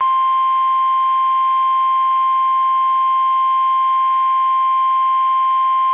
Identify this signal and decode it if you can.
Emergency Alert System (SAME) tones (the 1050 Hz attention tone). It is heard from 0.2 to 4.8 s.
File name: eas / KUSO / KUSO